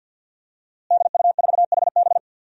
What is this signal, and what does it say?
Morse code, heard as a keyed tone, about 52 wpm, 700 Hz. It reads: DU4HB
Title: DXCC-100 Call Signs - 5 Characters - Repeated 3x - 50wpm